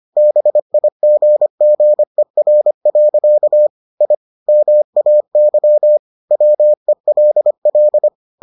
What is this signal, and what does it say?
Morse code, heard as a keyed tone, about 25 wpm, 600 Hz. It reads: BIGGER. I MAY WELL